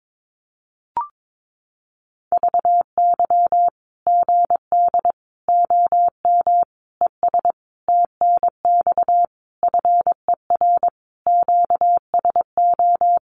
Morse code, heard as a keyed tone, about 22 wpm, 700 Hz. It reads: VY GD OM ES TNX FER QSO
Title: Common QSO Phrases 22wpm